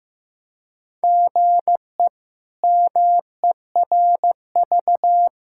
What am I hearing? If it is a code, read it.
Morse code, heard as a keyed tone, about 15 wpm, 700 Hz. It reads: GE MERV
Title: K1USN SST QSO Practice - 15wpm